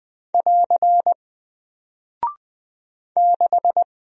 Morse code, heard as a keyed tone, about 20 wpm, 700 Hz. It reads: <AR> 6
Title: Single Character (Letter:Number:Punctuation:Prosign) 20wpm